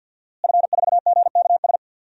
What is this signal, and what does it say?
Morse code, heard as a keyed tone, about 52 wpm, 700 Hz. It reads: F4GKS
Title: DXCC-100 Call Signs - 5 Characters - Repeated 2x - 50wpm